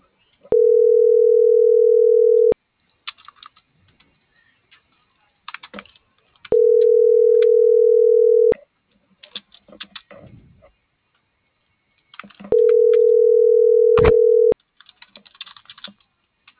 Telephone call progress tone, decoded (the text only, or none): ringback tone